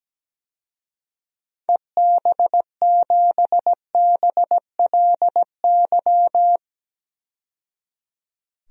Morse code, heard as a keyed tone, about 17 wpm, 700 Hz. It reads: EB7BLY
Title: DXCC-100 Call Signs - 6 Characters - 17wpm